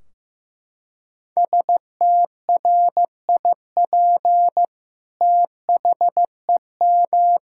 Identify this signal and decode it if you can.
Morse code, heard as a keyed tone, about 15 wpm, 700 Hz. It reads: STRIP THEM
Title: Sentences from Top 1600 Words 15wpm